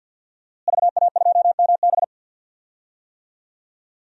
Morse code, heard as a keyed tone, about 46 wpm, 700 Hz. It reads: VA2MB
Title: DXCC-100 Call Signs - 5 Characters - Repeated 3x - 50wpm